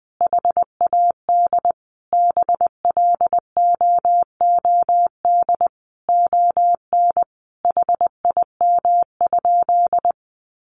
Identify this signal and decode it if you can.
Morse code, heard as a keyed tone, about 20 wpm, 700 Hz. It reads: HAD BLOOD ON HIM?
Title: Sentences from Top 1100 Words 20wpm